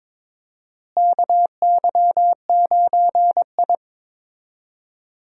Morse code, heard as a keyed tone, about 21 wpm, 700 Hz. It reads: KY9I
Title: USA Call Signs - 4 Characters - Repeated 3x - 22wpm